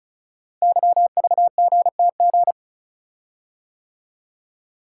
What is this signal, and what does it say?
Morse code, heard as a keyed tone, about 36 wpm, 700 Hz. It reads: YVGTG
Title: Code Groups - 5 Letters - 35wpm